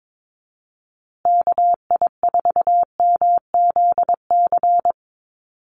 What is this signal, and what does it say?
Morse code, heard as a keyed tone, about 22 wpm, 700 Hz. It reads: KI4MZC
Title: Call Signs Repeated 2x - 22wpm